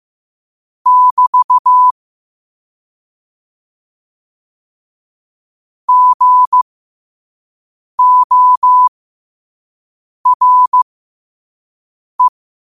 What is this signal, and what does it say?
Morse code, heard as a keyed tone, about 15 wpm, spaced as for 5 wpm, 1000 Hz. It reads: <BT> GORE
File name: 5wpm